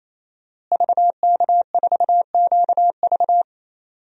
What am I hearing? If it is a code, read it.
Morse code, heard as a keyed tone, about 28 wpm, 700 Hz. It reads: VK4QV
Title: DXCC-100 Call Signs - 5 Characters - Repeated 3x - 28wpm